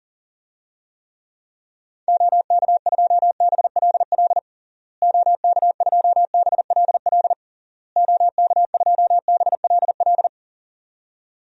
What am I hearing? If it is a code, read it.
Morse code, heard as a keyed tone, about 37 wpm, 700 Hz. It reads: OK2BLL OK2BLL OK2BLL
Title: DXCC-100 Call Signs - 6 Characters - Repeated 3x - 40wpm